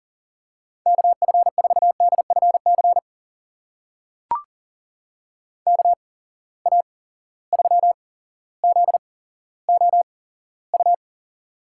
Morse code, heard as a keyed tone, about 40 wpm, spaced as for 17 wpm, 700 Hz. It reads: KF4DFC KA3ZOU
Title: USA Call Signs - 6 Characters - 8x Character Spacing - 40wpm